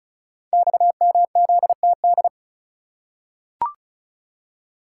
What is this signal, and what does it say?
Morse code, heard as a keyed tone, about 36 wpm, 700 Hz. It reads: XMZTD
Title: Code Groups - 5 Letter:Numbers - 35wpm